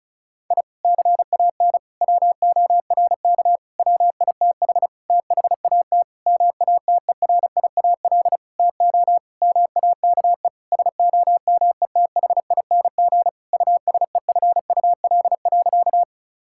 Morse code, heard as a keyed tone, about 36 wpm, 690 Hz. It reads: I CAN WORK WITH THAT MATERIAL TO MAKE SOMETHING USEFUL.
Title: Sentences from Top 1000 Words - Review Part 5 - 35wpm